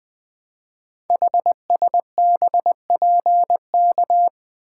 Morse code, heard as a keyed tone, about 20 wpm, 700 Hz. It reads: HSBPK
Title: Code Groups - 5 Letters - 20wpm